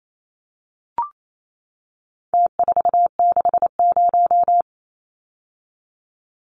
Morse code, heard as a keyed tone, about 28 wpm, 700 Hz. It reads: T460
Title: Numbers - Contest Style - 4-Digit 28wpm